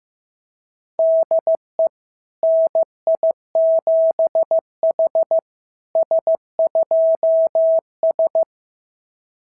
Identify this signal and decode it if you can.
Morse code, heard as a keyed tone, about 15 wpm, 690 Hz. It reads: DE NI7H S2S
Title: SOTA QSO Practice 15wpm